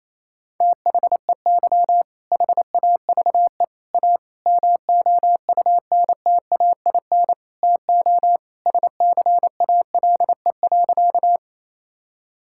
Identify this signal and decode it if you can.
Morse code, heard as a keyed tone, about 27 wpm, 700 Hz. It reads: THEY HAVE A MOUNTAIN TO SCALE.
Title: Sentences from Top 1000 Words - Review Part 5 - 28wpm